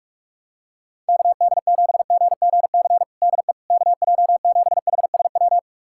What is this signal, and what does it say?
Morse code, heard as a keyed tone, about 46 wpm, 700 Hz. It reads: KD7GGC DE KJ7HSW
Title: USA Callsign DE Callsign - 6 Characters - 45wpm